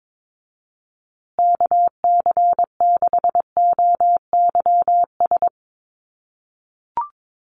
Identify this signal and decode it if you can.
Morse code, heard as a keyed tone, about 22 wpm, 700 Hz. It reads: KC6OYS
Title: USA Call Signs Repeated 2x 22wpm